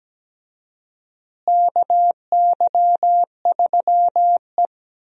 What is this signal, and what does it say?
Morse code, heard as a keyed tone, about 17 wpm, 700 Hz. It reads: KY3E
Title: USA Call Signs - 4 Characters - Repeated 2x - 17wpm